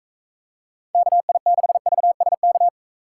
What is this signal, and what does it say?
Morse code, heard as a keyed tone, about 41 wpm, 700 Hz. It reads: KI6VSK